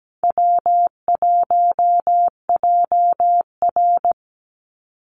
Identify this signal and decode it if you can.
Morse code, heard as a keyed tone, about 17 wpm, 700 Hz. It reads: W1JR
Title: Call Signs Repeated 2x - 17wpm